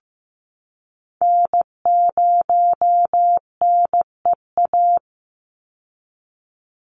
Morse code, heard as a keyed tone, about 15 wpm, 700 Hz. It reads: N0NEA
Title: USA Call Signs 15wpm